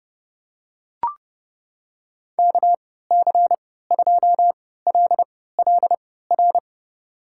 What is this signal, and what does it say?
Morse code, heard as a keyed tone, about 30 wpm, spaced as for 17 wpm, 700 Hz. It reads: KC2LLR
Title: USA Call Signs - 6 Characters - 3x Character Spacing - 30wpm